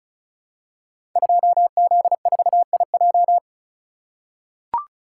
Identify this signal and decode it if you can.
Morse code, heard as a keyed tone, about 36 wpm, 700 Hz. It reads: 2Z4IJ
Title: Code Groups - 5 Letter:Numbers - 35wpm